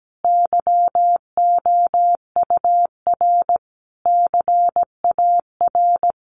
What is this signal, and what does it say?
Morse code, heard as a keyed tone, about 17 wpm, 700 Hz. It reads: YOUR CAR
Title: Sentences from Top 1100 Words 17wpm